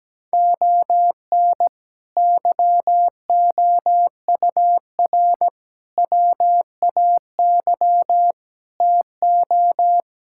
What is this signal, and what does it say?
Morse code, heard as a keyed tone, about 17 wpm, 700 Hz. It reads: ON YOUR WAY TO
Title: Sentences from Top 1000 Words - Review Part 6 - 17wpm